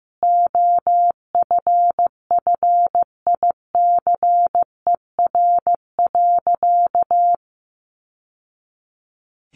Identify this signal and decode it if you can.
Morse code, heard as a keyed tone, about 15 wpm, 700 Hz. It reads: OFFICER.
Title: Sentences from Top 1200 Words 15wpm